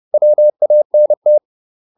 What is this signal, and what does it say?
Morse code, heard as a keyed tone, about 30 wpm, 600 Hz. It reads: WANT